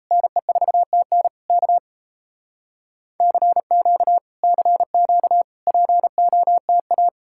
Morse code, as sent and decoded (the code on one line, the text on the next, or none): -. ....- - -. / -.- / -.-. --.- / -.-. --.- / .--. --- - .-
NE4TN K CQ CQ POTA